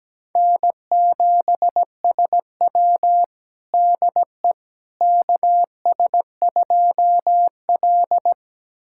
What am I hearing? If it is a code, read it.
Morse code, heard as a keyed tone, about 17 wpm, 700 Hz. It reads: N7SW DE KS2L
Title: USA Callsign DE Callsign - 4 Characters - 17wpm